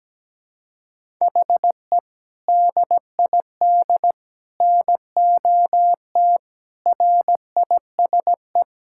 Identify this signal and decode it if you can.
Morse code, heard as a keyed tone, about 17 wpm, 700 Hz. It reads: HE DID NOT RISE
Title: Sentences from Top 1000 Words - Review Part 6 - 17wpm